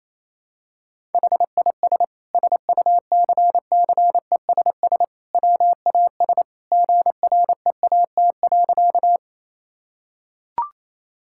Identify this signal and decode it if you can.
Morse code, heard as a keyed tone, about 28 wpm, 700 Hz. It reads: HIS SUCCESS WAS GREAT.
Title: Sentences from Top 1000 Words - Review Part 6 - 28wpm